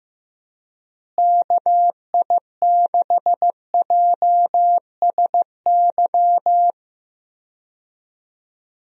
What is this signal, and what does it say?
Morse code, heard as a keyed tone, about 15 wpm, 700 Hz. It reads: KI6JSY
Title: USA Call Signs - 6 Characters - Repeated 2x - 15wpm